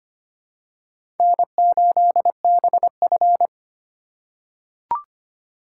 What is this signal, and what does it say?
Morse code, heard as a keyed tone, about 25 wpm, 700 Hz. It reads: N8BF
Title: USA Call Signs - 4x Character Spacing - 25wpm